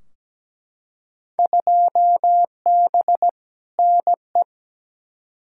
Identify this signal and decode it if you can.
Morse code, heard as a keyed tone, about 17 wpm, 700 Hz. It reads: 2B NE
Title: Field Day Exchanges Encore - 17wpm